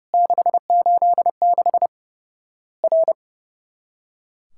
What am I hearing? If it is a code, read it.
Morse code, heard as a keyed tone, about 30 wpm, 700 Hz. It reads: 686 R
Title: Worked All Europe DX Contest - QTC Exchange - 30wpm